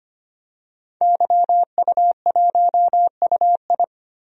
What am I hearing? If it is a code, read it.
Morse code, heard as a keyed tone, about 25 wpm, 690 Hz. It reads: YU1UI
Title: DXCC-100 Call Signs - 5 Characters - 25wpm